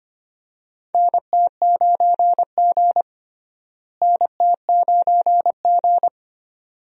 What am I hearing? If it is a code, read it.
Morse code, heard as a keyed tone, about 25 wpm, 700 Hz. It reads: NT9G NT9G
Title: USA Call Signs - 4 Characters - Repeated 2x - 25wpm